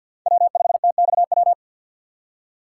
Morse code, heard as a keyed tone, about 47 wpm, 700 Hz. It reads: W5TXW
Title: USA Call Signs - 5 Characters - 4x Character Spacing - 50wpm